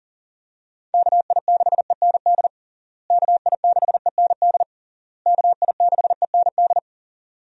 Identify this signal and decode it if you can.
Morse code, heard as a keyed tone, about 40 wpm, 700 Hz. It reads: KI6END KI6END KI6END
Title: USA Call Signs - 6 Characters - Repeated 3x - 40wpm